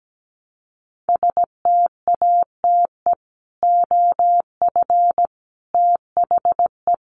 Morse code, heard as a keyed tone, about 17 wpm, 700 Hz. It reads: STATE OF THE
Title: Sets of 5 Words Encore 17wpm